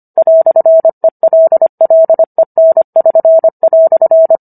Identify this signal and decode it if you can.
Morse code, heard as a keyed tone, about 25 wpm, 650 Hz. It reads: "ELLEN<SN>"